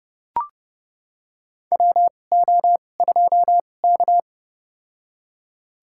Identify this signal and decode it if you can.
Morse code, heard as a keyed tone, about 30 wpm, spaced as for 22 wpm, 700 Hz. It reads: WO2K
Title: USA Call Signs - 4 Characters - 2x Character Spacing - 30wpm